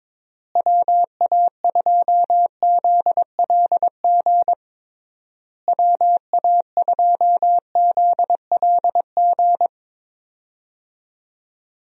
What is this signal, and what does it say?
Morse code, heard as a keyed tone, about 22 wpm, 700 Hz. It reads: WA2ZLG WA2ZLG
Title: USA Call Signs - 6 Characters - Repeated 3x - 22wpm